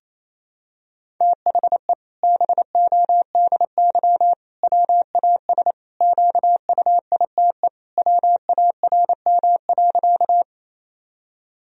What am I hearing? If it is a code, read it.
Morse code, heard as a keyed tone, about 27 wpm, 700 Hz. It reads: THE BODY WAS QUITE WARM.
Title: Sentences from Top 1000 Words - Review Part 2 - 28wpm